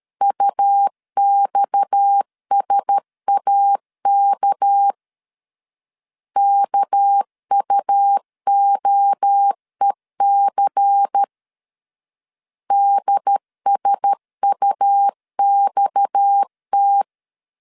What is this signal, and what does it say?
Morse code, heard as a keyed tone, about 13 wpm, 800 Hz. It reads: UXSAK KUOEC DSUXT
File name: LEZIONI / 6 PKX0 / 6 PKX 12wpm 2024